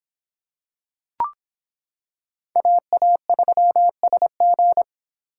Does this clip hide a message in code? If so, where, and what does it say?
Yes: Morse code from 2.6 to 4.8 s, heard as a keyed tone, about 26 wpm, 700 Hz. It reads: AA3SG